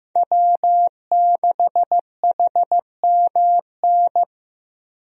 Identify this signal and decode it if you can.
Morse code, heard as a keyed tone, about 16 wpm, 700 Hz. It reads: W6HMN
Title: USA Call Signs - 5 Characters - Repeated 3x - 15wpm